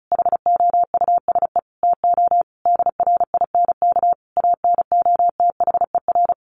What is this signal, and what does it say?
Morse code, heard as a keyed tone, about 34 wpm, 700 Hz. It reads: HOUSE TO DRINK ANOTHER
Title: Sentences from Top 1500 Words 35wpm